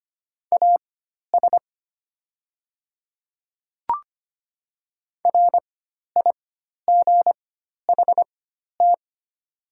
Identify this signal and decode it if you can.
Morse code, heard as a keyed tone, about 25 wpm, spaced as for 12 wpm, 700 Hz. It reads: AS RIGHT
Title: 4x Character Spacing - Top 200 Words - 25wpm